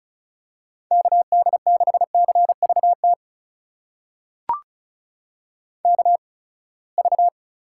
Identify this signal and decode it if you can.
Morse code, heard as a keyed tone, about 36 wpm, spaced as for 24 wpm, 700 Hz. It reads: KD6CVT KV
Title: USA Call Signs - 8x Character Spacing - 35wpm